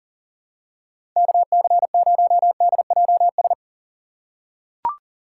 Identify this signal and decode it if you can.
Morse code, heard as a keyed tone, about 41 wpm, 700 Hz. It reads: KC0DJS
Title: USA Call Signs - 6 Characters - 4x Character Spacing - 40wpm